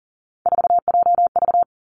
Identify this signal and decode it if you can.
Morse code, heard as a keyed tone, about 38 wpm, 700 Hz. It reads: 4JV
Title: Three Challenging Letter-Number Combinations 40wpm